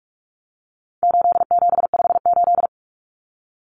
Morse code, heard as a keyed tone, about 44 wpm, 700 Hz. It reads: 8758
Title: Numbers - 4-Digit 45wpm